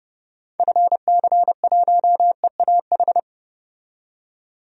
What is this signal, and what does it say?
Morse code, heard as a keyed tone, about 29 wpm, 700 Hz. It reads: FC1EAH
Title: DXCC-100 Call Signs - 6 Characters - 30wpm